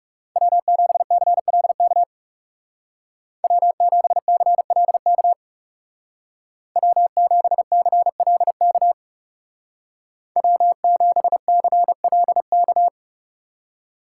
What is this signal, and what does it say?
Morse code, heard as a keyed tone, about 35 wpm, 700 Hz. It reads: W7CLK W7CLK W7CLK W7CLK
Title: Speed Racing - USA 1x3 Call Signs 30wpm